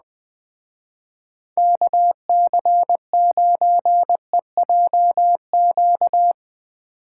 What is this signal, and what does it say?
Morse code, heard as a keyed tone, about 20 wpm, 700 Hz. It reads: KC9EJQ
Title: Speed Racing - USA Call Signs 17wpm